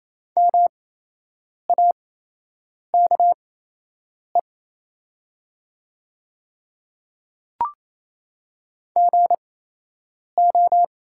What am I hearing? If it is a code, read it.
Morse code, heard as a keyed tone, about 28 wpm, spaced as for 8 wpm, 700 Hz. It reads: MAKE GO
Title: 8x Character Spacing - Top 100 Words - 28wpm